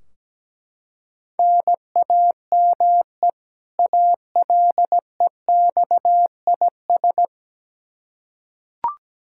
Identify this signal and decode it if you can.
Morse code, heard as a keyed tone, about 17 wpm, 700 Hz. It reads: NAME ALEXIS
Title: Practice QSO - Exchange 1 - Beginning - 17wpm